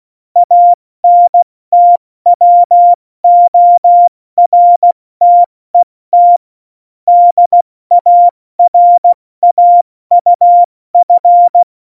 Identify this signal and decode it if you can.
Morse code, heard as a keyed tone, about 16 wpm, 700 Hz. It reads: ANTWORTET DARAUF
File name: Morsekurs / Witze / Witz 16. Juli 2019, 70 F80